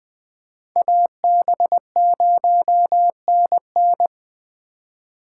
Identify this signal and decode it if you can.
Morse code, heard as a keyed tone, about 21 wpm, 700 Hz. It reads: AB0NN
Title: Speed Racing - USA 2x2 Call Signs 17wpm